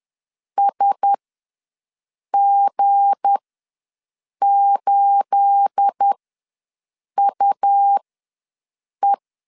Morse code, heard as a keyed tone, about 10 wpm, spaced as for 6 wpm, 800 Hz. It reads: SG8UE
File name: LEZIONI / 9 28 ws 10wpm 4124